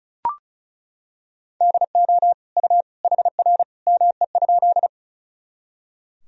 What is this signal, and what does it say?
Morse code, heard as a keyed tone, about 34 wpm, 700 Hz. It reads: DO U HR ME?
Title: Sentences from Top 300 Words and Abbreviations 35wpm